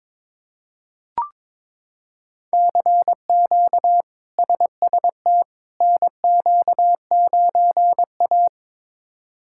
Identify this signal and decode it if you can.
Morse code, heard as a keyed tone, about 21 wpm, 700 Hz. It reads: CQ SST NQ9A
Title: K1USN SST QSO Practice - 22wpm